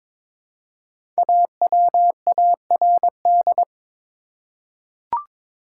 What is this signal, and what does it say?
Morse code, heard as a keyed tone, about 22 wpm, 700 Hz. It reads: AWARD
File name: 5 Letter Words - First 3 Letters Given - 22wpm